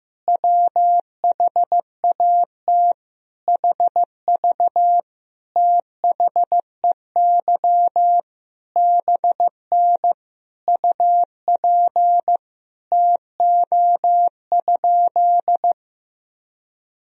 Morse code, heard as a keyed tone, about 15 wpm, 700 Hz. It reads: WHAT HV THEY BN UP TO?
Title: Sentences from Top 100 Words and Abbreviations 15wpm